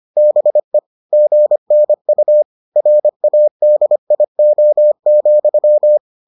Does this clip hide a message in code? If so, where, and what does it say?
Yes: Morse code from 0.2 to 6.0 s, heard as a keyed tone, about 25 wpm, 600 Hz. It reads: BE GNU RADIO,